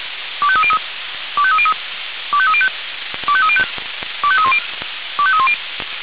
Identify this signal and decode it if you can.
Five-tone selective call (ZVEI), heard as quick runs of five tones, tones of about 70 ms. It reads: 25303 25303 25305 25305 25310 25310